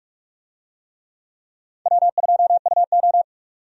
Morse code, heard as a keyed tone, about 43 wpm, 700 Hz. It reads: W2UO
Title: USA Call Signs - 4 Characters - 45wpm